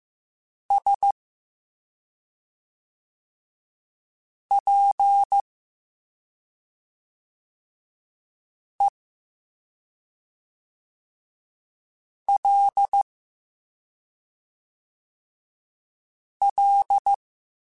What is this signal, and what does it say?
Morse code, heard as a keyed tone, about 15 wpm, spaced as for 2 wpm, 800 Hz. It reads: SPELL